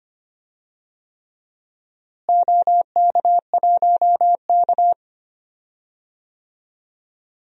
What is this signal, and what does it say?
Morse code, heard as a keyed tone, about 25 wpm, 700 Hz. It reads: OK1K
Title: DXCC-100 Call Signs - 4 Characters - Repeated 3x - 25wpm